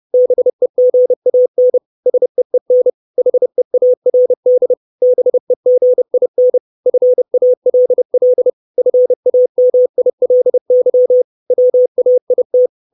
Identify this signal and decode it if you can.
Morse code, heard as a keyed tone, about 30 wpm, 500 Hz. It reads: BEGAN SEEN HEARD BEGIN FALL FAMILY WAIT
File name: 632 Words 30WPM